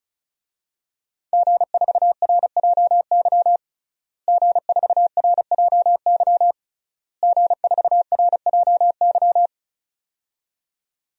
Morse code, heard as a keyed tone, about 36 wpm, 690 Hz. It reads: G4RJY G4RJY G4RJY
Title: DXCC-100 Call Signs - 5 Characters - Repeated 3x - 35wpm